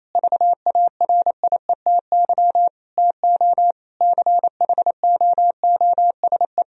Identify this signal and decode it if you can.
Morse code, heard as a keyed tone, about 28 wpm, 700 Hz. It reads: VARIETY TO CHOOSE
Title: Sentences from Top 1000 Words - Review Part 4 - 28wpm